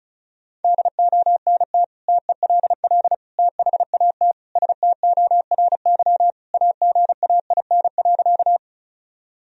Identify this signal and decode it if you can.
Morse code, heard as a keyed tone, about 36 wpm, 700 Hz. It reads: DONT TELL THAT STORY AGAIN.
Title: Sentences from Top 1000 Words - Review Part 1 - 35wpm